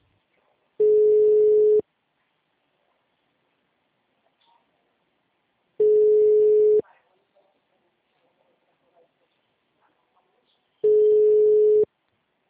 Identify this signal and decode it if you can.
Ringback tone (Europe, 425 Hz), with speech.